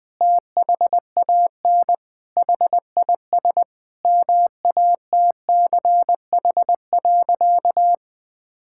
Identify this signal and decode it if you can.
Morse code, heard as a keyed tone, about 20 wpm, 700 Hz. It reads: THAN HIS MATCH.
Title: Sentences from Top 1600 Words 20wpm